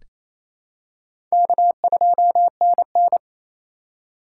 Morse code, heard as a keyed tone, about 28 wpm, 700 Hz. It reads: K2NN
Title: USA Call Signs - 4 Characters - 2x Character Spacing - 28wpm